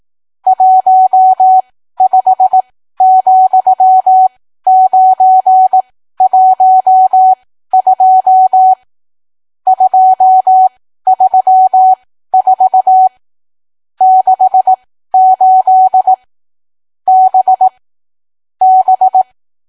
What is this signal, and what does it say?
Morse code, heard as a keyed tone, about 18 wpm, spaced as for 13 wpm, 750 Hz. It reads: 15,912 234 68 B B